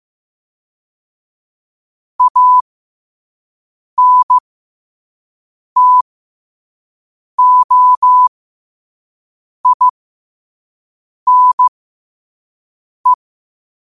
Morse code, heard as a keyed tone, about 15 wpm, spaced as for 5 wpm, 1000 Hz. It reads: ANTOINE
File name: qotdfiles / 5wpm